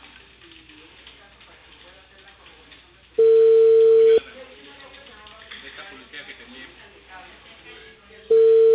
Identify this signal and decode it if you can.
Ringback tone (China, 450 Hz), with speech.